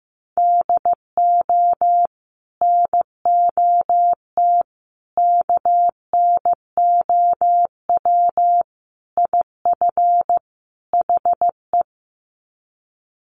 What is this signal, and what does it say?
Morse code, heard as a keyed tone, about 15 wpm, 700 Hz. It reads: DO NOT KNOW IF HE
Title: Sets of 5 Words Encore2 15wpm